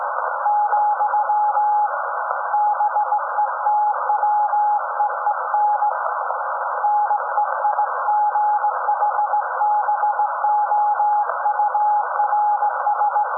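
Morse code, heard as a keyed tone, about 18 wpm, 840 Hz, with noise. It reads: , BETWEEN NEIGHBOURS
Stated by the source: SAQ Grimeton Radio, Alexanderson 200 kW alternator on 17.2 kHz, received in Potsdam with a very strong signal on World Radio Day.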